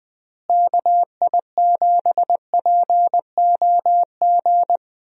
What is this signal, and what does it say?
Morse code, heard as a keyed tone, about 20 wpm, 700 Hz. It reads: KI7POG
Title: USA Call Signs - 6 Characters - Repeated 3x - 20wpm